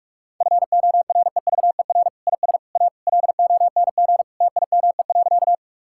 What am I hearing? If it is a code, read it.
Morse code, heard as a keyed tone, about 42 wpm, 700 Hz. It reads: FOREVER IS A LONG TIME.
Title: Sentences from Top 1600 Words 45wpm